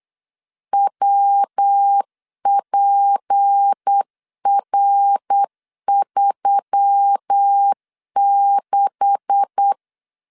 Morse code, heard as a keyed tone, about 8 wpm, 800 Hz. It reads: WPR36
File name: LEZIONI / 10 SIGNS / 10A SLASH 8wpm 3452